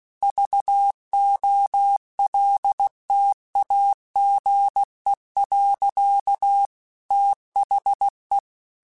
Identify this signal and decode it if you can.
Morse code, heard as a keyed tone, about 16 wpm, 800 Hz. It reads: VOLTAGE. THE